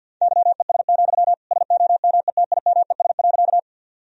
Morse code, heard as a keyed tone, about 50 wpm, 700 Hz. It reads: YES, SOMETIMES.
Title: Sentences from Top 1000 Words - Review Part 3 - 50wpm